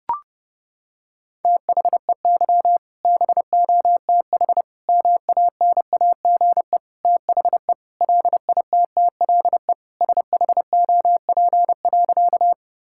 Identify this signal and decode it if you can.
Morse code, heard as a keyed tone, about 30 wpm, 700 Hz. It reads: THEY BOTH MANAGE THE LITTLE SHOP.
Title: Sentences from Top 1600 Words 30wpm